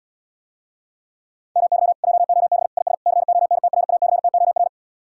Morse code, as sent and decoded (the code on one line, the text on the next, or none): -- -.-- / .--- --- -... / .. ... / -.-. --- -. - .-. .- -.-. - --- .-.
MY JOB IS CONTRACTOR